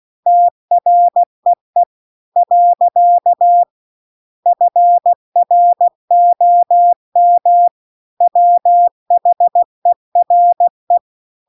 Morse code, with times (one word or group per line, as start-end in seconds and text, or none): 0.3-1.8 TREE
2.4-3.6 .
4.5-7.7 FROM
8.2-11.0 WHERE